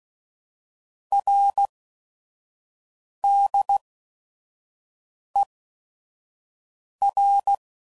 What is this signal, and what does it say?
Morse code, heard as a keyed tone, about 16 wpm, spaced as for 5 wpm, 800 Hz. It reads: RDER